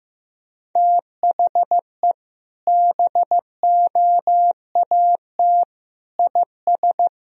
Morse code, heard as a keyed tone, about 16 wpm, 700 Hz. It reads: THE BOAT IS
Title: Sentences from Top 1000 Words - Review Part 2 - 15wpm